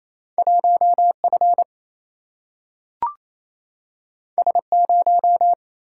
Morse code, heard as a keyed tone, about 28 wpm, 700 Hz. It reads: 1 F S 0
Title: Single-Letter Single-Number 28wpm